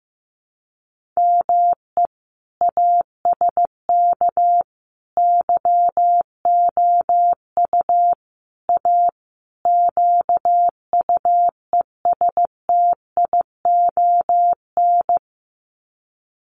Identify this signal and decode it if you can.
Morse code, heard as a keyed tone, about 15 wpm, 700 Hz. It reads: ME ASK YOU A QUESTION
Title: Sets of 5 Words Encore2 15wpm